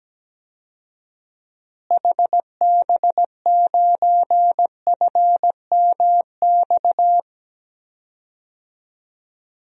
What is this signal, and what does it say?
Morse code, heard as a keyed tone, about 17 wpm, 700 Hz. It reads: HB9FMX
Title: DXCC-100 Call Signs - 6 Characters - Repeated 2x - 17wpm